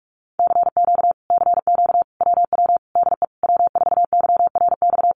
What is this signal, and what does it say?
Morse code, heard as a keyed tone, about 46 wpm, 700 Hz. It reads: CQ CQ WW DE W4YRX
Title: Common QSO Phrases Encore 45wpm